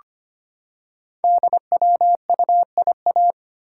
Morse code, heard as a keyed tone, about 25 wpm, 690 Hz. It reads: DWUIA
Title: Code Groups - 5 Letter:Numbers - 25wpm